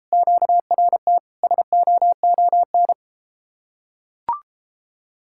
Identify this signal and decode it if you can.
Morse code, heard as a keyed tone, about 33 wpm, 700 Hz. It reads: QRT SOON